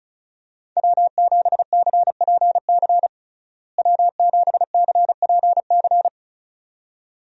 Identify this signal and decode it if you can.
Morse code, heard as a keyed tone, about 34 wpm, 700 Hz. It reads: W7CPC W7CPC
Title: USA Call Signs - 5 Characters - Repeated 2x - 35wpm